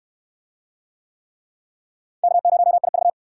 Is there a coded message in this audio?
Yes: Morse code from 2.2 to 3.1 s, heard as a keyed tone, about 66 wpm, 700 Hz. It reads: K0IV